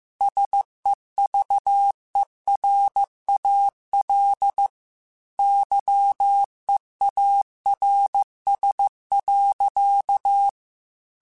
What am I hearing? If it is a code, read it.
Morse code, heard as a keyed tone, about 15 wpm, 800 Hz. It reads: SEVERAL YEARS.
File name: QSOs / QSOs-mp3s / kc7sac@13